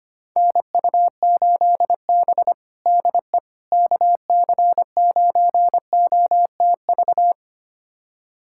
Morse code, heard as a keyed tone, about 25 wpm, 700 Hz. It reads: NU8B DE KC9OTV
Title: USA Callsign DE Callsign 25wpm